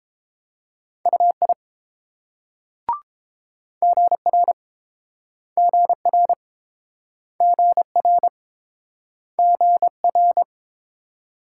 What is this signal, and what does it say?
Morse code, heard as a keyed tone, about 28 wpm, 700 Hz. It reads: UI GR GR GR GR